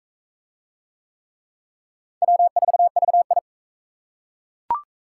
Morse code, heard as a keyed tone, about 39 wpm, 700 Hz. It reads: W4VI